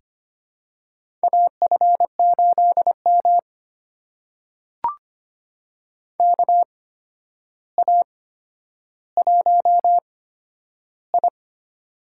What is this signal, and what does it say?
Morse code, heard as a keyed tone, about 25 wpm, spaced as for 11 wpm, 700 Hz. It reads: AF8M KA1I
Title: USA Call Signs - 4 Characters - 8x Character Spacing - 25wpm